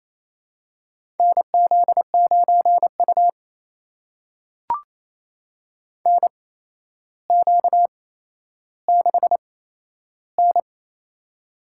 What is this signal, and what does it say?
Morse code, heard as a keyed tone, about 28 wpm, spaced as for 12 wpm, 700 Hz. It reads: NZ9U NQ6N
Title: USA Call Signs - 8x Character Spacing - 28wpm